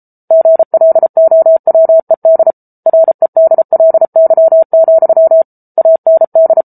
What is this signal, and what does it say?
Morse code, heard as a keyed tone, about 34 wpm, 650 Hz. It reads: GLOWED REDLY, AND